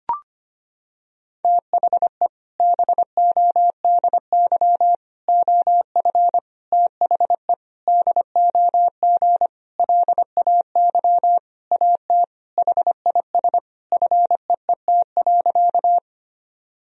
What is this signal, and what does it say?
Morse code, heard as a keyed tone, about 25 wpm, 700 Hz. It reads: THE BODY OF THE DOG LAY AT HIS FEET.
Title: Sentences from Top 1000 Words - Review Part 2 - 25wpm